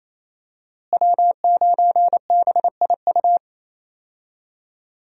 Morse code, heard as a keyed tone, about 28 wpm, 700 Hz. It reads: W9BIU
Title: USA Call Signs - 5 Characters - Repeated 3x - 28wpm